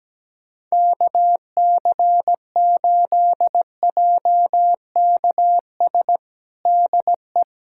Morse code, heard as a keyed tone, about 17 wpm, 700 Hz. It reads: KC8JKS DE
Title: USA Callsign DE Callsign - 6 Characters - 17wpm